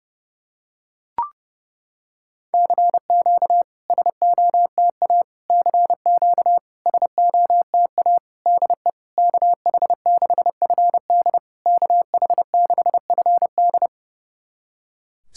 Morse code, heard as a keyed tone, about 31 wpm, 690 Hz. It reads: CQ SOTA CQ SOTA DE KH6FD KH6FD
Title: SOTA QSO Practice 30wpm